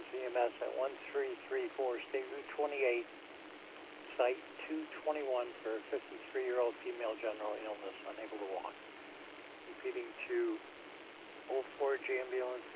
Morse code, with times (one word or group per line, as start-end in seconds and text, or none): none